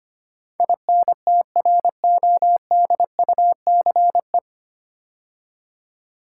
Morse code, heard as a keyed tone, about 25 wpm, 700 Hz. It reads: INTRODUCE
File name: practice / 25wpm / 9 Letter Words - First 3 Letters Given - 25wpm